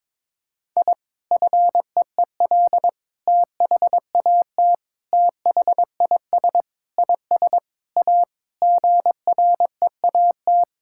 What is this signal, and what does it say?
Morse code, heard as a keyed tone, about 22 wpm, 700 Hz. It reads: I FEEL THAT THIS IS A GREAT
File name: practice / 22wpm / Sentences with 4-Word Phrases - 22wpm